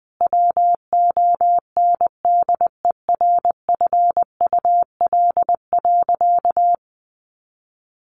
Morse code, heard as a keyed tone, about 20 wpm, 700 Hz. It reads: WONDERFUL.
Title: Sentences from Top 1400 Words 20wpm